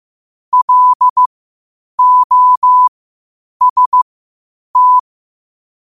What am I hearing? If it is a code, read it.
Morse code, heard as a keyed tone, about 15 wpm, spaced as for 8 wpm, 1000 Hz. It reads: LOST
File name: qotdfiles / 8wpm